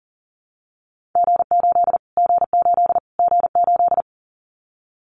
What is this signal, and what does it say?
Morse code, heard as a keyed tone, about 40 wpm, 700 Hz. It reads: G8 G8 G8
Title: Callsign Prefixes for Top 25 DXCC - Repeated 3x - 40wpm